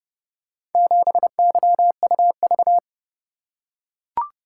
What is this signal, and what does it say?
Morse code, heard as a keyed tone, about 29 wpm, 700 Hz. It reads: 7YUV